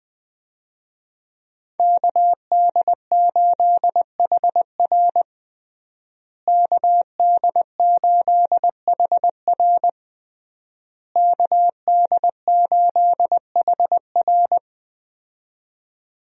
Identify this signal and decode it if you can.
Morse code, heard as a keyed tone, about 21 wpm, 700 Hz. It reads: KD8HR KD8HR KD8HR
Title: USA Call Signs - 5 Characters - Repeated 3x - 20wpm